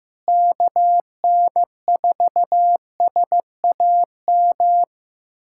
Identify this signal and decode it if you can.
Morse code, heard as a keyed tone, about 15 wpm, 700 Hz. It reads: KN4SAM
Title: USA Callsign DE Callsign 15wpm